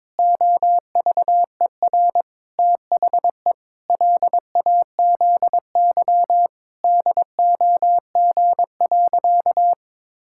Morse code, heard as a keyed tone, about 21 wpm, 700 Hz. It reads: OVER THE LAZY DOG.
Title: Pangram Challenge v2 - 22wpm